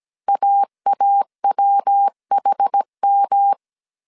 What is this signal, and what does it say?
Morse code, heard as a keyed tone, about 17 wpm, 800 Hz. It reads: AAWHM